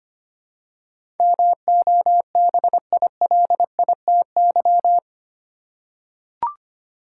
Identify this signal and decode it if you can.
Morse code, heard as a keyed tone, about 25 wpm, 700 Hz. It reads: MOBILITY
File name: 8 Letter Words - First 3 Letters Given - 25wpm